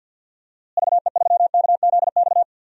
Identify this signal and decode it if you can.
Morse code, heard as a keyed tone, about 46 wpm, 700 Hz. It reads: VE3KZX
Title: DXCC-100 Call Signs - 6 Characters - Repeated 3x - 50wpm